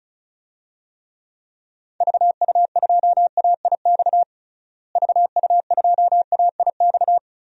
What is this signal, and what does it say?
Morse code, heard as a keyed tone, about 36 wpm, 700 Hz. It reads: VU2AIX VU2AIX
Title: DXCC-100 Call Signs - 6 Characters - Repeated 2x - 35wpm